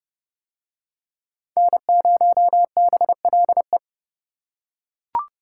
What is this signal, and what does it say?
Morse code, heard as a keyed tone, about 28 wpm, 700 Hz. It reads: N0BLE